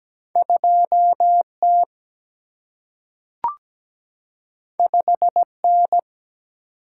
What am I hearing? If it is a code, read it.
Morse code, heard as a keyed tone, about 17 wpm, 700 Hz. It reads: 2T 5N
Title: Single-Letter Single-Number 17wpm